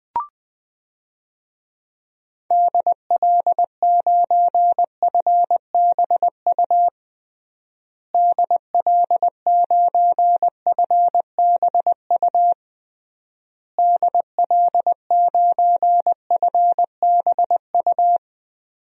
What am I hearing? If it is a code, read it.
Morse code, heard as a keyed tone, about 19 wpm, 700 Hz. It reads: DL9FBU DL9FBU DL9FBU
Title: DXCC-100 Call Signs - 6 Characters - Repeated 3x - 20wpm